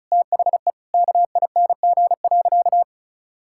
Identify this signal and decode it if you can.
Morse code, heard as a keyed tone, about 34 wpm, 700 Hz. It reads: THE KING.
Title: Sentences from Top 1000 Words - Review Part 6 - 35wpm